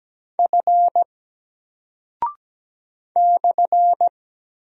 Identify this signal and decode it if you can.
Morse code, heard as a keyed tone, about 17 wpm, 700 Hz. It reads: F /
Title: Single Character (Letter:Number:Punctuation:Prosign) 17wpm